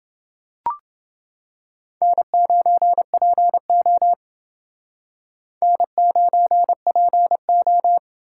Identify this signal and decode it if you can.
Morse code, heard as a keyed tone, about 30 wpm, 700 Hz. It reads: N9PO N9PO